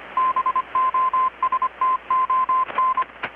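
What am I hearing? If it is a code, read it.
Morse code, heard as a keyed tone, about 25 wpm, 1040 Hz, with radio noise. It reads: BOSTON
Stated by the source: Morse from KPH's Night of Nights XX maritime radiotelegraphy event, received in Chiapas, Mexico about 3,800 km away on a Kenwood R-600 with a grounded 10 m long-wire antenna.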